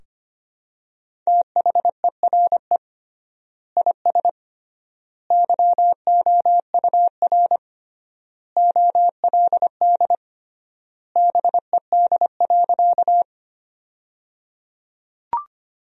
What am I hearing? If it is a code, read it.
Morse code, heard as a keyed tone, about 25 wpm, 700 Hz. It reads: THERE IS YOUR OLD BED.
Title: 3x Word-Spacing - Sentences from Top 500 Words - 25wpm